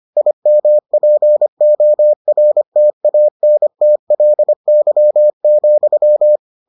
Morse code, heard as a keyed tone, about 25 wpm, 600 Hz. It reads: IMPORTANTLY,